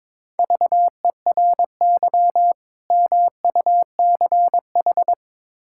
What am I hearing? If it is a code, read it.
Morse code, heard as a keyed tone, about 22 wpm, 700 Hz. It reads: VERY MUCH
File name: practice / 22wpm / Build A Phrase - 5-Word Phrases - 22wpm